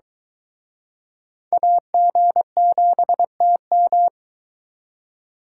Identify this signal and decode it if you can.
Morse code, heard as a keyed tone, about 24 wpm, 700 Hz. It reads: AG7TM